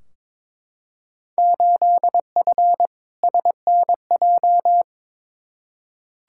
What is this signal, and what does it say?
Morse code, heard as a keyed tone, about 22 wpm, 700 Hz. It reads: 8F SNJ
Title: Field Day Exchanges Encore - 22wpm